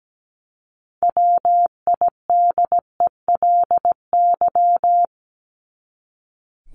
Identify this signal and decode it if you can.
Morse code, heard as a keyed tone, about 17 wpm, 700 Hz. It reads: WIDELY
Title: LY Suffix - Top 100 Words with Base Words 17wpm